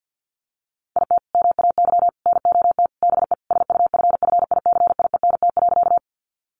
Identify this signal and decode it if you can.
Morse code, heard as a keyed tone, about 50 wpm, 700 Hz. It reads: IT MAY NOT BE SUFFICIENT.
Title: Sentences from Top 1400 Words 50wpm